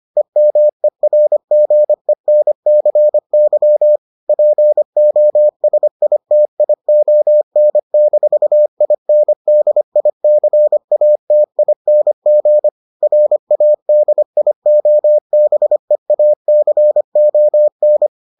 Morse code, heard as a keyed tone, about 25 wpm, 600 Hz. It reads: EMERGENCY POSITION-INDICATING RADIOBEACON